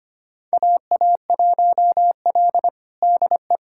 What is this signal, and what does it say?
Morse code, heard as a keyed tone, about 25 wpm, 700 Hz. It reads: AA1L DE
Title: USA Callsign DE Callsign - 4 Characters - 25wpm